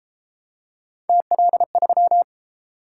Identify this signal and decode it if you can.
Morse code, heard as a keyed tone, about 28 wpm, 700 Hz. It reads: TL3